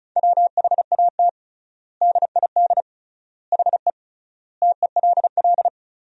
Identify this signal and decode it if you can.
Morse code, heard as a keyed tone, about 36 wpm, 700 Hz. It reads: WHAT DID HE TELL
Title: 3x Word-Spacing - Sentences from Top 200 Words - 35wpm